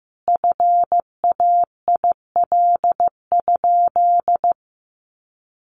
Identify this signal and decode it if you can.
Morse code, heard as a keyed tone, about 15 wpm, 700 Hz. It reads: FAIL?
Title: Sentences from Top 1100 Words 15wpm